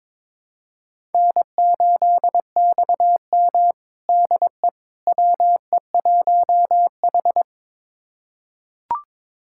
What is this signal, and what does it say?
Morse code, heard as a keyed tone, about 22 wpm, 700 Hz. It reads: N8XM DE WE1H
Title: USA Callsign DE Callsign - 4 Characters - 22wpm